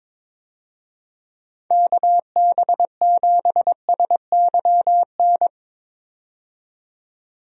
Morse code, heard as a keyed tone, about 22 wpm, 700 Hz. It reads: KB7SYN
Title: USA Call Signs - 6 Characters - 22wpm